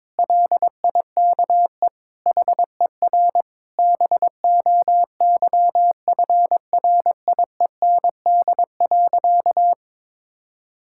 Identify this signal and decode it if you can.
Morse code, heard as a keyed tone, about 21 wpm, 700 Hz. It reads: LIKE HER BOYFRIEND.
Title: Sentences from Top 1000 Words - Review Part 5 - 22wpm